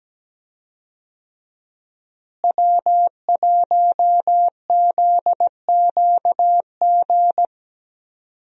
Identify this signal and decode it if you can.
Morse code, heard as a keyed tone, about 17 wpm, 700 Hz. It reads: W1ZQG
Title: USA Call Signs - 5 Characters - Repeated 3x - 17wpm